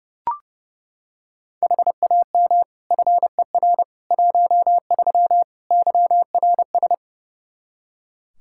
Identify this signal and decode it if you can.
Morse code, heard as a keyed tone, about 32 wpm, 700 Hz. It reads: HAM FER 13 YRS
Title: Practice QSO - Exchange 3 - More Info - 30wpm